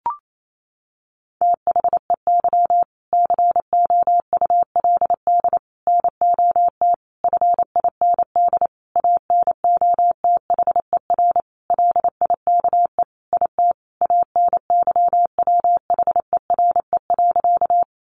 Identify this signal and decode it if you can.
Morse code, heard as a keyed tone, about 28 wpm, 700 Hz. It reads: THEY COULD NOT FIND ANOTHER LIKE IT ANYWHERE.
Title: Sentences from Top 1400 Words 28wpm